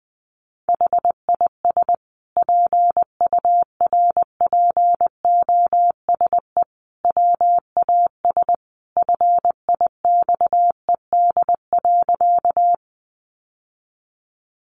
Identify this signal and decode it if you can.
Morse code, heard as a keyed tone, about 20 wpm, 700 Hz. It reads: HIS PURPOSE WAS FIXED.
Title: Sentences from Top 1100 Words 20wpm